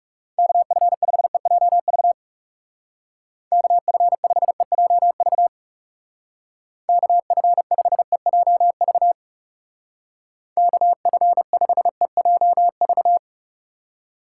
Morse code, heard as a keyed tone, about 36 wpm, 700 Hz. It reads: KF5EJV KF5EJV KF5EJV KF5EJV
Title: Speed Racing - USA 2x3 Call Signs 30wpm